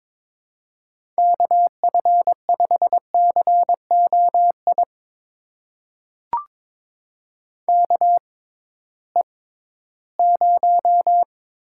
Morse code, heard as a keyed tone, about 21 wpm, spaced as for 14 wpm, 700 Hz. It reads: KF5COI KE0
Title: USA Call Signs - 6 Characters - 6x Character Spacing - 22wpm